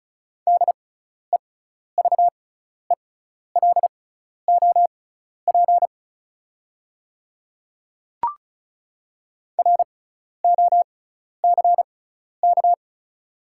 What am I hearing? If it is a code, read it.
Morse code, heard as a keyed tone, about 34 wpm, spaced as for 12 wpm, 690 Hz. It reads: DEVELOP ROCK
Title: 6x Character Spacing - Top 500 Words - 35wpm